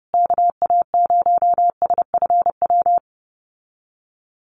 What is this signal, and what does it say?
Morse code, heard as a keyed tone, about 28 wpm, 700 Hz. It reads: KA0SFW
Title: USA Call Signs Repeated 3x 30wpm